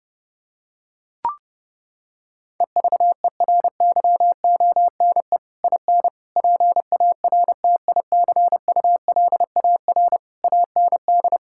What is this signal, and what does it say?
Morse code, heard as a keyed tone, about 32 wpm, 700 Hz. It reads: EVERYONE IN PARTICULAR AND
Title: Sentences from Top 1600 Words 30wpm